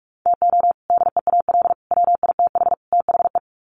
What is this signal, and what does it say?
Morse code, heard as a keyed tone, about 44 wpm, 700 Hz. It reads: TO DEAL WITH THE
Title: Sets of 4 Words 45wpm